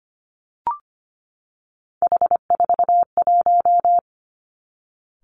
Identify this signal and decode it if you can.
Morse code, heard as a keyed tone, about 25 wpm, 700 Hz. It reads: H41
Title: Three Challenging Letter-Number Combinations 25wpm